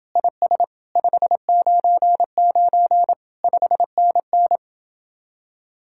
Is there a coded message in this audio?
Yes: Morse code from 0.2 to 4.6 s, heard as a keyed tone, about 28 wpm, 700 Hz. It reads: IS 599 5NN